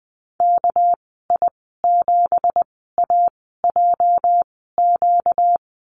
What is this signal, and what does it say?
Morse code, heard as a keyed tone, about 20 wpm, spaced as for 14 wpm, 700 Hz. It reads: KI7AJQ
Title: 245 Lesson 39g - Letter z - Call Signs 20wpm